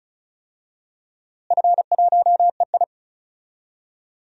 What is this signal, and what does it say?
Morse code, heard as a keyed tone, about 34 wpm, 700 Hz. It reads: F1EI
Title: DXCC-100 Call Signs - 4 Characters - Repeated 2x - 35wpm